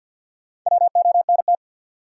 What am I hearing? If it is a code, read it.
Morse code, heard as a keyed tone, about 53 wpm, 700 Hz. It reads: WONT
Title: Sentences from Top 1000 Words - Review Part 6 - New Word Drill - 50wpm